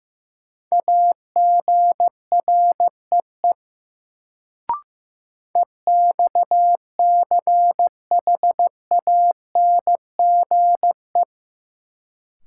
Morse code, heard as a keyed tone, about 16 wpm, 700 Hz. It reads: AGREE EXCHANGE
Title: Sentences from Top 1000 Words - Review Part 6 - New Word Drill - 15wpm